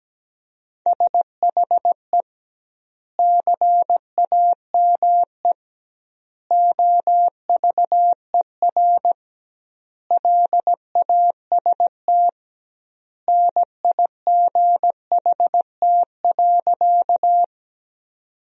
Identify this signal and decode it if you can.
Morse code, heard as a keyed tone, about 17 wpm, 700 Hz. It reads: SHE CAME OVER LAST NIGHT.
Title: 2x Word-Spacing - Sentences from Top 300 Words - 17wpm